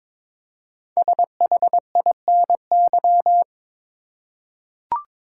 Morse code, heard as a keyed tone, about 22 wpm, 700 Hz. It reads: SHINY
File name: 5 Letter Words - First 3 Letters Given - 22wpm